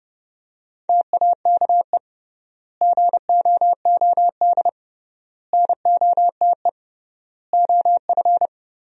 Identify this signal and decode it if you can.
Morse code, heard as a keyed tone, about 29 wpm, 700 Hz. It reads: TAKE GOOD NOTE OF
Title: 3x Word-Spacing - Sentences from Top 500 Words - 30wpm